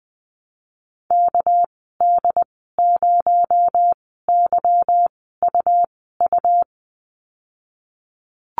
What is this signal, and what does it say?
Morse code, heard as a keyed tone, about 19 wpm, spaced as for 14 wpm, 700 Hz. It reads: KD0YUU
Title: 219 Lesson 35g - Number 0 - Call Signs 20wpm